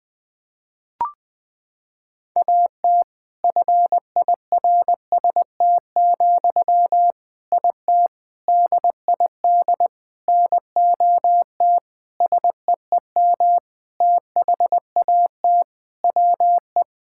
Morse code, heard as a keyed tone, about 21 wpm, 700 Hz. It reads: AT FIRST, IT DID NOT SEEM THAT WE
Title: Sentences from Top 1000 Words - Review Part 6 - 20wpm